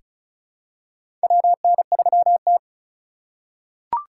Morse code, heard as a keyed tone, about 36 wpm, 700 Hz. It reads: WN3T
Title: USA Call Signs - 4 Characters - 4x Character Spacing - 35wpm